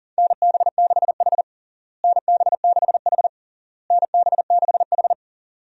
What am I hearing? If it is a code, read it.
Morse code, heard as a keyed tone, about 40 wpm, 700 Hz. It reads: NB6H NB6H NB6H
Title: USA Call Signs - 4 Characters - Repeated 3x - 40wpm